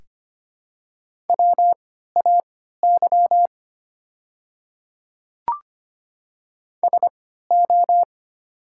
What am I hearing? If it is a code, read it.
Morse code, heard as a keyed tone, about 25 wpm, spaced as for 14 wpm, 700 Hz. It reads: WAY SO
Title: 3x Character Spacing - Top 200 Words - 25wpm